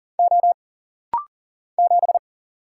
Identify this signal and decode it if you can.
Morse code, heard as a keyed tone, about 40 wpm, 700 Hz. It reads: O 7
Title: Single Letter-Number Alt - Rapid-Fire 40wpm